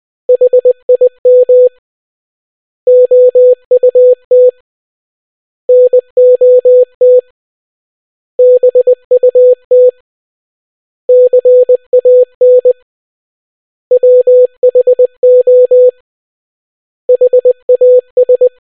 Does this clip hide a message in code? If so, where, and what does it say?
Yes: Morse code from 0.3 to 18.5 s, heard as a keyed tone, about 20 wpm, 500 Hz. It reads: HIM OUT NOT BUT CAN WHO HAS